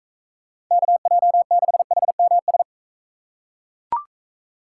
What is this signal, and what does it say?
Morse code, heard as a keyed tone, about 41 wpm, 700 Hz. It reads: KJ6HMS